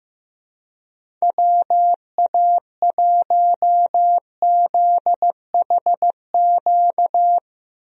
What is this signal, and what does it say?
Morse code, heard as a keyed tone, about 15 wpm, 700 Hz. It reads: WA1ZHQ
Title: USA Call Signs - 6 Characters - 15wpm